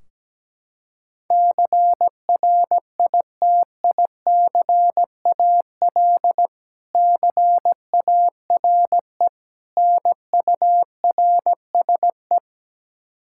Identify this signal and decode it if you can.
Morse code, heard as a keyed tone, about 17 wpm, 700 Hz. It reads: CRITICAL CARE NURSE
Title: K1USN SST QSO Practice 17wpm